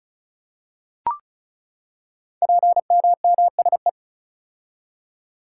Morse code, heard as a keyed tone, about 36 wpm, 700 Hz. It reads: PMMSE
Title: Code Groups - 5 Letters - 35wpm